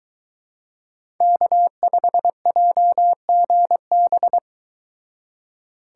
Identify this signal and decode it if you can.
Morse code, heard as a keyed tone, about 23 wpm, 700 Hz. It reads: K5JGB